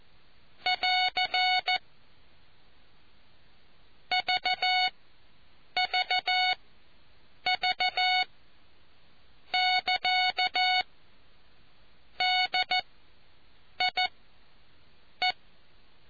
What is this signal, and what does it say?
Morse code, heard as a keyed tone, about 14 wpm, spaced as for 7 wpm, 740 Hz. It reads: <AR> VVV <KA> DIE